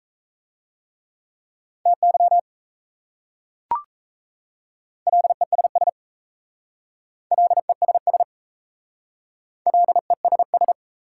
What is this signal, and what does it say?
Morse code, heard as a keyed tone, about 35 wpm, 700 Hz. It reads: TY LESS LESS LESS